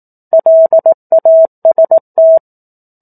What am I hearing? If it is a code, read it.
Morse code, heard as a keyed tone, about 18 wpm, 650 Hz. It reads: LAST
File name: 04-Chapter-04 18 WPM